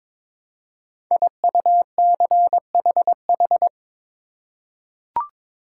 Morse code, heard as a keyed tone, about 22 wpm, 700 Hz. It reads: IUCHH
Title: Code Groups - 5 Letters - 22wpm